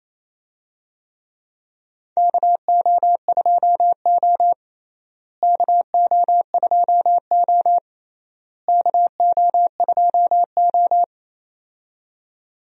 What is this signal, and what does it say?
Morse code, heard as a keyed tone, about 28 wpm, 700 Hz. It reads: KO2O KO2O KO2O
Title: USA Call Signs - 4 Characters - Repeated 3x - 28wpm